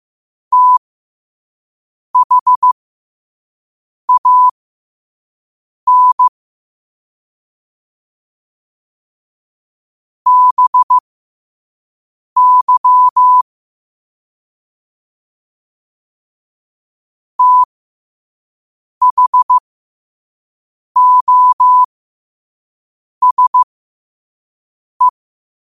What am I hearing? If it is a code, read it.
Morse code, heard as a keyed tone, about 15 wpm, spaced as for 5 wpm, 1000 Hz. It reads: THAN BY THOSE
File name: qotdfiles / 5wpm